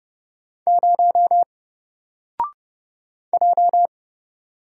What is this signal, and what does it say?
Morse code, heard as a keyed tone, about 30 wpm, 700 Hz. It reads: 0 J
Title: Single Letter-Number 30wpm